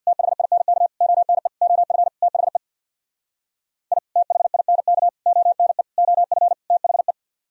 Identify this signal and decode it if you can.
Morse code, heard as a keyed tone, about 65 wpm, 700 Hz. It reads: THINK ONE OF THE I THINK ONE OF THE